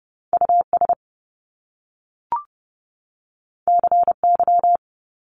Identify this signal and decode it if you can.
Morse code, heard as a keyed tone, about 30 wpm, 700 Hz. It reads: US CY
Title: Top Two Letter Combinations in Words 30wpm